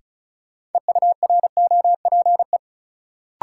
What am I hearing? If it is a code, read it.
Morse code, heard as a keyed tone, about 36 wpm, 700 Hz. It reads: EUROPE
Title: Sets of 5 Words Encore - New Word Drill - 35wpm